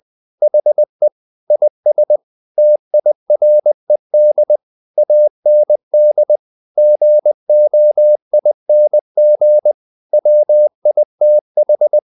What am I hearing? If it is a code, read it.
Morse code, heard as a keyed tone, about 20 wpm, 600 Hz. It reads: HE IS TIRED AND GOING WITH
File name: the old man and the sea part 2 20wpm